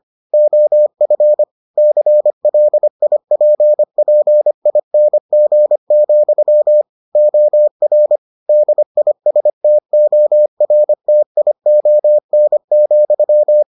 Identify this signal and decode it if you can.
Morse code, heard as a keyed tone, about 25 wpm, 600 Hz. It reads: OF CLIPPING, OR DISTORTION,